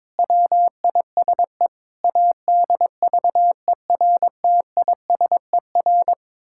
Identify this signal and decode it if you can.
Morse code, heard as a keyed tone, about 22 wpm, 700 Hz. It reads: WISE ADVERTISER
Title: Pangram Challenge v2 - 22wpm